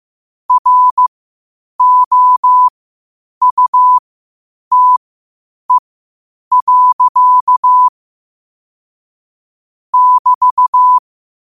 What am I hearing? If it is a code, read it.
Morse code, heard as a keyed tone, about 15 wpm, spaced as for 8 wpm, 1000 Hz. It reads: ROUTE. <BT>
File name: qotdfiles / 8wpm